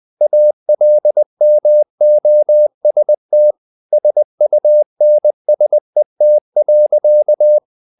Morse code, heard as a keyed tone, about 20 wpm, 600 Hz. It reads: ALMOST SUNSET.
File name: the old man and the sea part 2 20wpm